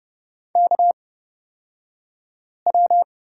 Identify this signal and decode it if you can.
Morse code, heard as a keyed tone, about 31 wpm, spaced as for 5 wpm, 700 Hz. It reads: KW
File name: Lesson 38d - Letter q - Two Characters 30wpm